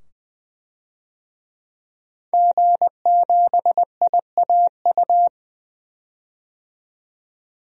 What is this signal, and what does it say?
Morse code, heard as a keyed tone, about 20 wpm, 700 Hz. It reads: G7IAU
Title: DXCC-100 Call Signs - 5 Characters - 20wpm